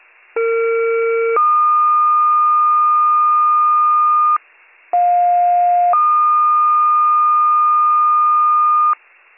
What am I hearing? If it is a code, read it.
Two-tone page, heard as a one-second tone, then a three-second tone, about 460 then 1220 Hz, about 710 then 1220 Hz. It is heard from 0.4 to 8.9 s.